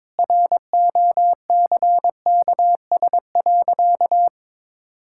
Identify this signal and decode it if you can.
Morse code, heard as a keyed tone, about 22 wpm, 700 Hz. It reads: ROCKS.
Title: Pangram Challenge v2 - 22wpm